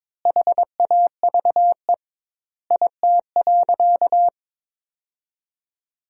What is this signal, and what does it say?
Morse code, heard as a keyed tone, about 22 wpm, 700 Hz. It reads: HAVE IT.
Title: 2x Word-Spacing - Sentences from Top 300 Words - 22wpm